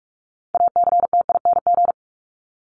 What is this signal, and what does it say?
Morse code, heard as a keyed tone, about 44 wpm, 700 Hz. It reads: ACTING
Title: ING Suffix - Top 100 Words with Base Words 45wpm